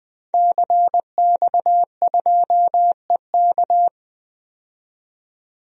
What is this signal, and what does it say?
Morse code, heard as a keyed tone, about 20 wpm, 700 Hz. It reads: CX2EK
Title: DXCC-100 Call Signs - 5 Characters - 20wpm